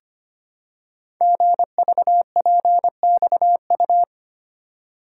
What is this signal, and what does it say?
Morse code, heard as a keyed tone, about 25 wpm, 700 Hz. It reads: GVPXU
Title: Code Groups - 5 Letters - 25wpm